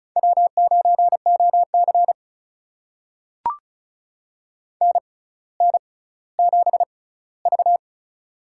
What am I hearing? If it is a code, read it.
Morse code, heard as a keyed tone, about 36 wpm, spaced as for 18 wpm, 700 Hz. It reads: W9OC NN7V
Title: USA Call Signs - 6x Character Spacing - 35wpm